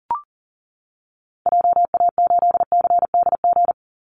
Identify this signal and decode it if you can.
Morse code, heard as a keyed tone, about 37 wpm, 700 Hz. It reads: JA8CDG
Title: Call Signs Repeated 2x - 40wpm